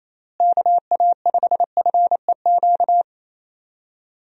Morse code, heard as a keyed tone, about 28 wpm, 700 Hz. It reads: KA5FEQ
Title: USA Call Signs - 6 Characters - Repeated 3x - 28wpm